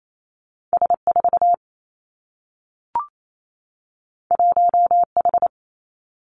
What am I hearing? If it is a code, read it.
Morse code, heard as a keyed tone, about 28 wpm, 700 Hz. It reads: S 4 1 H
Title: Two Challenging Letter-Number Combinations 28wpm